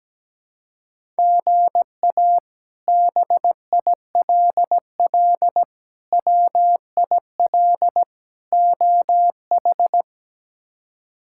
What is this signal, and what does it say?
Morse code, heard as a keyed tone, about 17 wpm, 700 Hz. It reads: GA BILL WIL OH
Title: K1USN SST QSO Practice - 17wpm